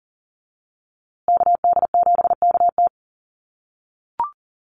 Morse code, heard as a keyed tone, about 40 wpm, 700 Hz. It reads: KD7KT
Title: USA Call Signs Repeated 2x 40wpm